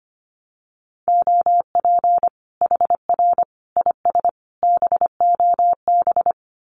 Morse code, heard as a keyed tone, about 25 wpm, 700 Hz. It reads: OP HR IS BOB
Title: Common QSO Phrases 25wpm